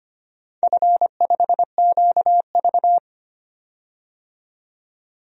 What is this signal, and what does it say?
Morse code, heard as a keyed tone, about 25 wpm, 690 Hz. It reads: F5QV
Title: DXCC-100 Call Signs - 4 Characters - 25wpm